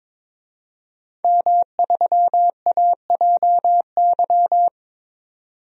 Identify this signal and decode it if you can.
Morse code, heard as a keyed tone, about 22 wpm, 700 Hz. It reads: M3AJY
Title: DXCC-100 Call Signs - 5 Characters - 22wpm